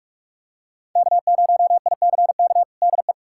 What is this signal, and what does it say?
Morse code, heard as a keyed tone, about 42 wpm, 700 Hz. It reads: K0ICK DE
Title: USA Callsign DE Callsign - 5 Characters - 45wpm